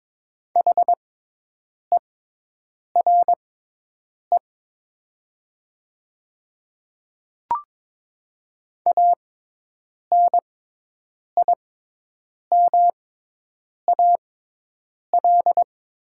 Morse code, heard as a keyed tone, about 22 wpm, spaced as for 8 wpm, 700 Hz. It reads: HERE ANIMAL
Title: 6x Character Spacing - Top 500 Words - 22wpm